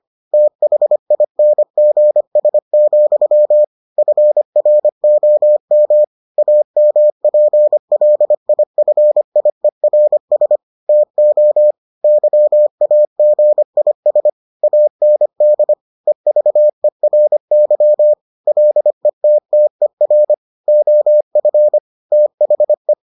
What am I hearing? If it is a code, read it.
Morse code, heard as a keyed tone, about 25 wpm, 600 Hz. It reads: THINGS, FROM AMPLIFIERS TO YAGIS AND EVERY LETTER OF THE